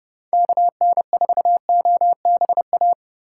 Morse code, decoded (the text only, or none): KN4OBA